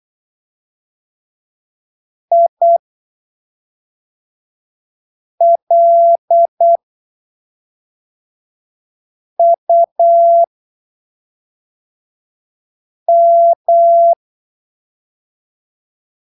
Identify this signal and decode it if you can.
Morse code, heard as a keyed tone, about 8 wpm, spaced as for 3 wpm, 680 Hz. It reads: ILUM
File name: arquivos / texto / texto 29 08ppm